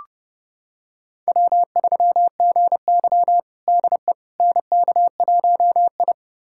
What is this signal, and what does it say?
Morse code, heard as a keyed tone, about 30 wpm, 700 Hz. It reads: W3GY DE NK1I
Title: USA Callsign DE Callsign - 4 Characters - 30wpm